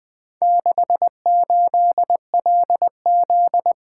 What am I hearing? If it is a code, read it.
Morse code, heard as a keyed tone, about 24 wpm, 700 Hz. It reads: 68LZ
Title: Speed-Racing - Four Letter-Number Combinations - 20wpm